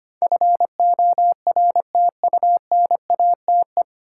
Morse code, heard as a keyed tone, about 25 wpm, 690 Hz. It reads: FORTUNATE
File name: practice / 25wpm / 9 Letter Words - First 3 Letters Given - 25wpm